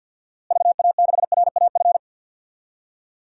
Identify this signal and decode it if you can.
Morse code, heard as a keyed tone, about 50 wpm, 700 Hz. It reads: VA6RAF
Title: DXCC-100 Call Signs - 6 Characters - Repeated 2x - 50wpm